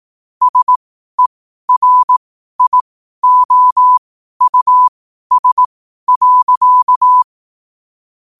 Morse code, heard as a keyed tone, about 18 wpm, spaced as for 12 wpm, 1000 Hz. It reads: SERIOUS.
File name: qotdfiles / 12wpm